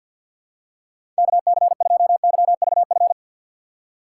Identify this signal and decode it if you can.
Morse code, heard as a keyed tone, about 53 wpm, 700 Hz. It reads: KC2YVF
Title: USA Call Signs - 6 Characters - 4x Character Spacing - 50wpm